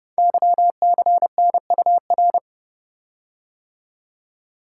Morse code, heard as a keyed tone, about 30 wpm, 700 Hz. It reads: YCNUR
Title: Code Groups - 5 Letter:Numbers - 30wpm